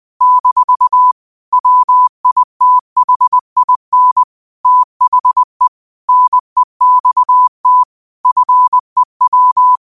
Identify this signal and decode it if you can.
Morse code, heard as a keyed tone, about 20 wpm, 1000 Hz. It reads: - WITHIN THE NEXT FEW